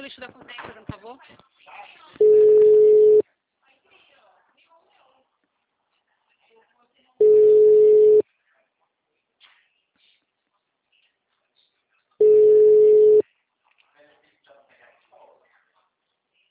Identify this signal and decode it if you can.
Ringback tone (Europe, 425 Hz), with speech.